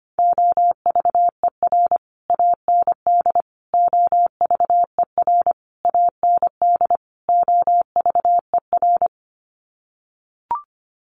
Morse code, heard as a keyed tone, about 25 wpm, 700 Hz. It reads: OVER AND OVER AND OVER
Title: Sets of 5 Words Encore 25wpm